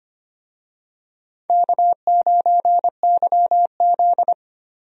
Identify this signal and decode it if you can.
Morse code, heard as a keyed tone, about 25 wpm, 700 Hz. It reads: K9YZ
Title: USA Call Signs - 4 Characters - 25wpm